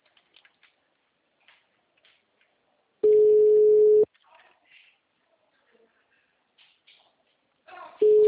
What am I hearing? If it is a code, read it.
Ringback tone (Europe, 425 Hz).
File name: out- -2002